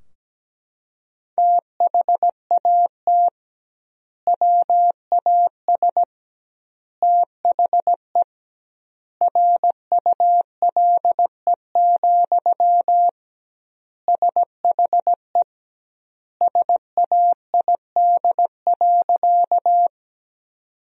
Morse code, heard as a keyed tone, about 17 wpm, 700 Hz. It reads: THAT WAS THE RULE, SHE SAID.
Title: 2x Word-Spacing - Sentences from Top 400 Words - 17wpm